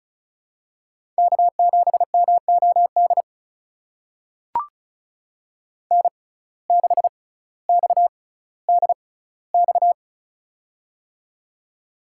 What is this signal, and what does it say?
Morse code, heard as a keyed tone, about 36 wpm, spaced as for 18 wpm, 700 Hz. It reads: K7MOD N6XDX
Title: USA Call Signs - 5 Characters - 6x Character Spacing - 35wpm